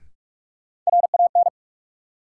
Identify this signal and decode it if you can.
Morse code, heard as a keyed tone, about 42 wpm, 700 Hz. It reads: RAN
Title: Sentences from Top 500 Words - New Word Drill - 45wpm